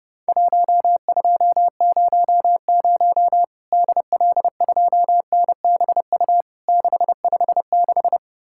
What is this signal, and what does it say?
Morse code, heard as a keyed tone, about 32 wpm, 700 Hz. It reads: 1200 DL2NBU 656
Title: Worked All Europe DX Contest - QTC Exchange - 30wpm